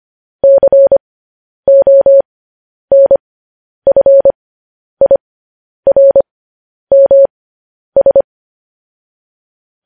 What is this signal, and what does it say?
Morse code, heard as a keyed tone, about 25 wpm, spaced as for 10 wpm, 550 Hz. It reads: CONFIRMS
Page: news-headlines-in-morse-code-at-10-wpm